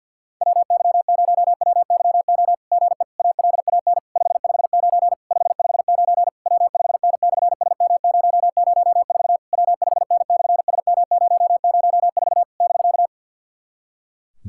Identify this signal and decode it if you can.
Morse code, heard as a keyed tone, about 50 wpm, 700 Hz. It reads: WY0WYO GE ALAN 559 559 W5N/SM004 W5N/SM004 <BK>
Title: SOTA QSO Practice 50wpm